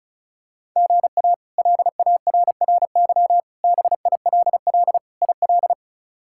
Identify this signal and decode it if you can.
Morse code, heard as a keyed tone, about 36 wpm, 700 Hz. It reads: GA LARRY BILL IL
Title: K1USN SST QSO Practice - 35wpm